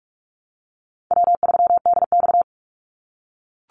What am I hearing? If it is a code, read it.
Morse code, heard as a keyed tone, about 44 wpm, 700 Hz. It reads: W3DX
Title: NA SOTA Activator Callsigns 45wpm